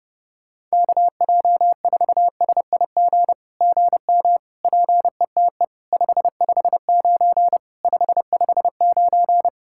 Morse code, heard as a keyed tone, about 30 wpm, 700 Hz. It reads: KJ4SIG GM PETE 559 559
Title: SOTA QSO Practice 30wpm